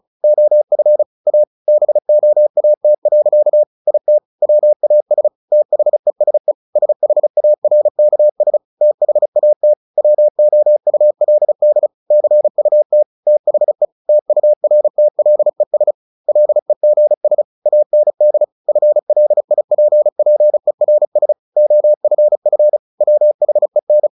Morse code, heard as a keyed tone, about 35 wpm, 600 Hz. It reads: OF A BOAT. IT WAS THESE SHARKS THAT WOULD CUT THE TURTLES LEGS AND FLIPPERS OFF WHEN
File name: the old man and the sea 2 35wpm